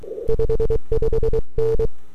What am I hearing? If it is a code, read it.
Morse code, heard as a keyed tone, about 23 wpm, 450 Hz. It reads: 55N
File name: n6tr / 12Oct2009-160